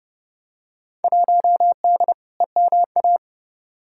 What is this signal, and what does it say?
Morse code, heard as a keyed tone, about 32 wpm, 700 Hz. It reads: 1D EMA
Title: Field Day Exchanges - 30wpm